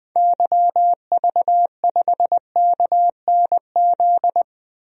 Morse code, heard as a keyed tone, about 21 wpm, 700 Hz. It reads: YV5KNZ
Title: DXCC-100 Call Signs - 6 Characters - Repeated 3x - 20wpm